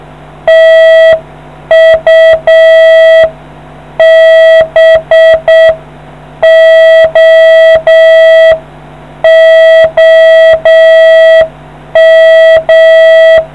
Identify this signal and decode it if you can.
Morse code, heard as a keyed tone, about 7 wpm, 650 Hz. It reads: TU BO OM